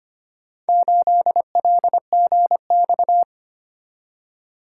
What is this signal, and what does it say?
Morse code, heard as a keyed tone, about 25 wpm, 700 Hz. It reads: 8LGX
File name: practice / 25wpm / Four Letter-Number Combinations - 25wpm